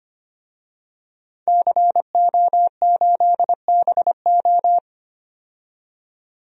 Morse code, heard as a keyed tone, about 25 wpm, 700 Hz. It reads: CO8BO
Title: DXCC-100 Call Signs - 5 Characters - 25wpm